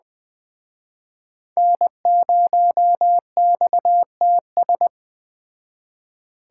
Morse code, heard as a keyed tone, about 22 wpm, 700 Hz. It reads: N0XTS